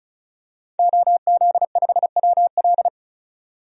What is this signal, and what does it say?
Morse code, heard as a keyed tone, about 34 wpm, 700 Hz. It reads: OZ5WL
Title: DXCC-100 Call Signs - 5 Characters - 35wpm